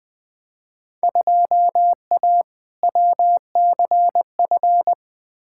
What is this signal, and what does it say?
Morse code, heard as a keyed tone, about 20 wpm, 700 Hz. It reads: 2A WCF
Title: Field Day Exchanges Encore - 20wpm